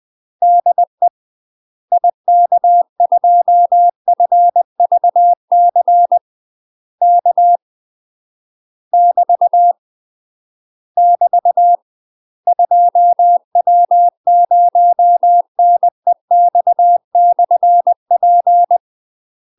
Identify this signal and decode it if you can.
Morse code, heard as a keyed tone, about 20 wpm, 700 Hz. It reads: DE IK2FVC K <BT> <BT> 2W0NEX/P
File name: MASTER QSOs TRAINING/20WPM